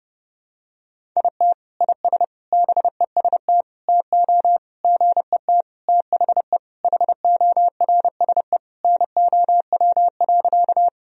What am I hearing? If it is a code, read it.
Morse code, heard as a keyed tone, about 30 wpm, 700 Hz. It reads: IT IS BEST TO GET THE HORSE NOW.
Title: Sentences from Top 1000 Words - Review Part 1 - 30wpm